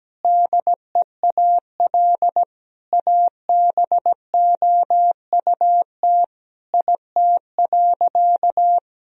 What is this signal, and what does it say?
Morse code, heard as a keyed tone, about 17 wpm, 700 Hz. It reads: DEAL ABOUT IT.
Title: Sentences from Top 1000 Words - Review Part 1 - 17wpm